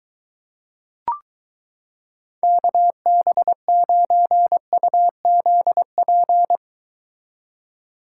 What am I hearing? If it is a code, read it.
Morse code, heard as a keyed tone, about 23 wpm, 700 Hz. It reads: KB9UZP